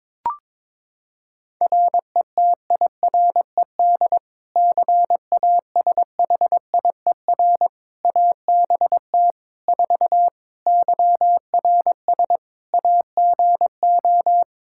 Morse code, heard as a keyed tone, about 22 wpm, 700 Hz. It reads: RETIRED CASHIER ABT 4 YRS AGO
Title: Practice QSO - Exchange 3 - More Info - 22wpm